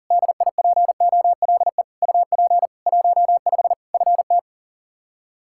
Morse code, heard as a keyed tone, about 37 wpm, 700 Hz. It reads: DIPOLE UP 15 FT
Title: Practice QSO - Exchange 2 - Info - 40wpm